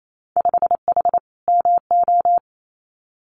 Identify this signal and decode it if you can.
Morse code, heard as a keyed tone, about 28 wpm, 700 Hz. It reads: 5H MO
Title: Winter Field Day Exchanges Encore 28wpm